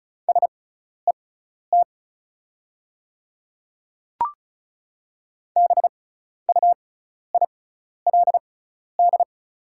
Morse code, heard as a keyed tone, about 36 wpm, spaced as for 12 wpm, 700 Hz. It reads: SET BUILD
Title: 6x Character Spacing - Top 200 Words - 35wpm